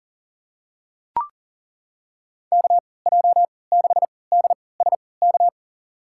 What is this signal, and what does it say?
Morse code, heard as a keyed tone, about 40 wpm, spaced as for 23 wpm, 700 Hz. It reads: KJ6DSK
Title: USA Call Signs - 3x Character Spacing - 40wpm